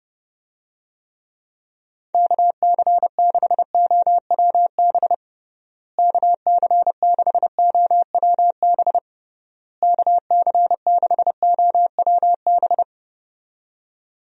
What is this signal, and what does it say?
Morse code, heard as a keyed tone, about 30 wpm, 700 Hz. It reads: KC6OWB KC6OWB KC6OWB
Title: USA Call Signs - 6 Characters - Repeated 3x - 30wpm